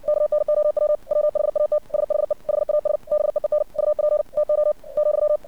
Morse code, heard as a keyed tone, about 59 wpm, 600 Hz. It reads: KNOW WHAT SHE HAS BEEN UP TO ,
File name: archived / CRSnet